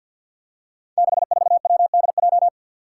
Morse code, heard as a keyed tone, about 52 wpm, 700 Hz. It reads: 64WDJ
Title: Five Challenging Letter-Number Combinations 50wpm